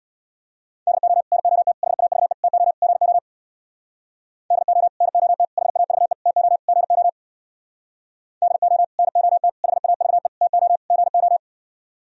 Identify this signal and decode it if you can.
Morse code, heard as a keyed tone, about 58 wpm, 700 Hz. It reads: DO NOT HAVE TO GO DO NOT HAVE TO GO DO NOT HAVE TO GO
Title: Speed-Racing - Sets of 5 Words 50wpm